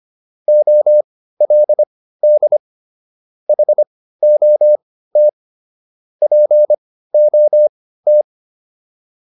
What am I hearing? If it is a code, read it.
Morse code, heard as a keyed tone, about 25 wpm, spaced as for 15 wpm, 600 Hz. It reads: OLD HOT POT